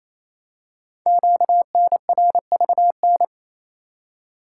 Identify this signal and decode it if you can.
Morse code, heard as a keyed tone, about 27 wpm, 700 Hz. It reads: QNRVN
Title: Code Groups - 5 Letters - 28wpm